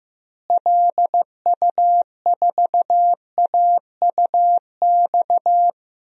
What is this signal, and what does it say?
Morse code, heard as a keyed tone, about 15 wpm, 700 Hz. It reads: LU4AUX
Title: DXCC-100 Call Signs - 6 Characters - Repeated 2x - 15wpm